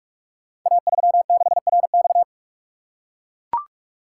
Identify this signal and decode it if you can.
Morse code, heard as a keyed tone, about 44 wpm, 690 Hz. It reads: A36RX
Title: Code Groups - 5 Letter:Numbers - 45wpm